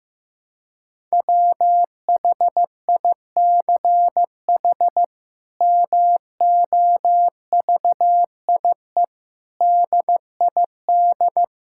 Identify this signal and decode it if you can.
Morse code, heard as a keyed tone, about 15 wpm, 700 Hz. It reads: WHICH MOVIE DID
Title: Sentences from Top 1000 Words - Review Part 5 - 15wpm